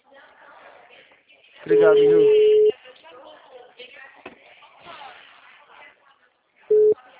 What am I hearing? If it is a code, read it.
Ringback tone (Europe, 425 Hz), with speech.
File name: out- -2012- - - .59